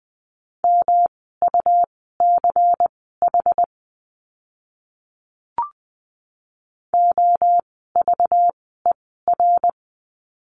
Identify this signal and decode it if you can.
Morse code, heard as a keyed tone, about 20 wpm, spaced as for 15 wpm, 700 Hz. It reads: MUCH OVER
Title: 237 Lesson 38f - Letter q - Words 20wpm